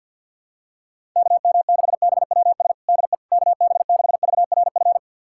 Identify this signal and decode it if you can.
Morse code, heard as a keyed tone, about 51 wpm, 700 Hz. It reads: KM6BWS DE KB6VRF
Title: USA Callsign DE Callsign - 6 Characters - 50wpm